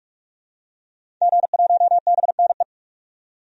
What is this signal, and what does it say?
Morse code, heard as a keyed tone, about 46 wpm, 700 Hz. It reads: G1BNE
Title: DXCC-100 Call Signs - 5 Characters - 45wpm